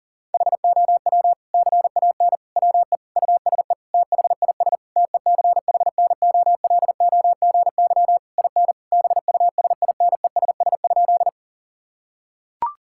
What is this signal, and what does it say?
Morse code, heard as a keyed tone, about 37 wpm, 700 Hz. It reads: HOW CAN WE USE THIS TECHNOLOGY IN BUSINESS?
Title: Sentences from Top 1000 Words - Review Part 5 - 40wpm